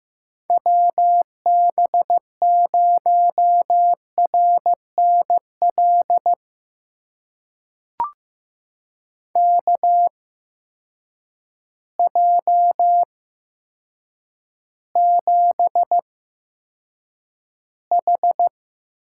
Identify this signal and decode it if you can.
Morse code, heard as a keyed tone, about 15 wpm, spaced as for 8 wpm, 700 Hz. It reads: WB0RNL KJ7H
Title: USA Call Signs - 6 Characters - 8x Character Spacing - 15wpm